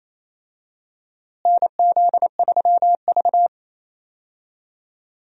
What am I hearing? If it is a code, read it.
Morse code, heard as a keyed tone, about 27 wpm, 700 Hz. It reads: NZ3V
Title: USA Call Signs - 4 Characters - Repeated 3x - 28wpm